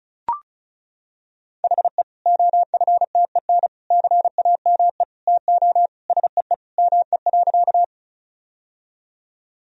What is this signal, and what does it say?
Morse code, heard as a keyed tone, about 36 wpm, 700 Hz. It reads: HE OFTEN CAME TO SEE ME.
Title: Sentences from Top 1000 Words - Review Part 1 - 35wpm